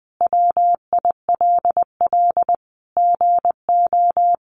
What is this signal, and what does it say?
Morse code, heard as a keyed tone, about 20 wpm, 700 Hz. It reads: WILL GO
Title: Sentences from Top 400 Words 20wpm